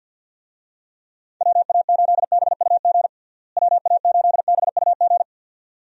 Morse code, heard as a keyed tone, about 47 wpm, 700 Hz. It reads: WA8BUG WA8BUG
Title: USA Call Signs - 6 Characters - Repeated 2x - 50wpm